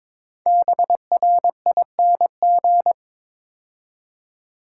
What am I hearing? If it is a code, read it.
Morse code, heard as a keyed tone, about 22 wpm, 700 Hz. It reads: BRING
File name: 5 Letter Words - First 3 Letters Given - 22wpm